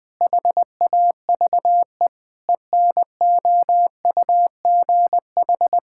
Morse code, heard as a keyed tone, about 21 wpm, 700 Hz. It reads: HAVE ENOUGH
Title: Sentences from Top 1000 Words - Review Part 5 - 20wpm